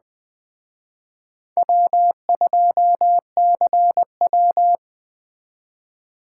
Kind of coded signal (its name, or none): Morse code